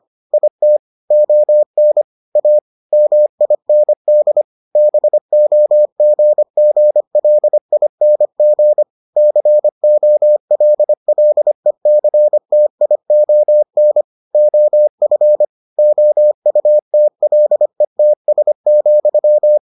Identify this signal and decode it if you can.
Morse code, heard as a keyed tone, about 25 wpm, 600 Hz. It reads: IT ON A MIND BOGGLING COLLECTION OF OUTLETS,